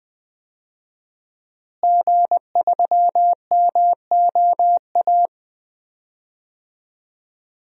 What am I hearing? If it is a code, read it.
Morse code, heard as a keyed tone, about 20 wpm, 700 Hz. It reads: G3MOA
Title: DXCC-100 Call Signs - 5 Characters - Repeated 3x - 20wpm